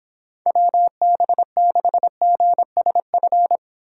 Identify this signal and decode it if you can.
Morse code, heard as a keyed tone, about 31 wpm, 700 Hz. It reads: WB6GSF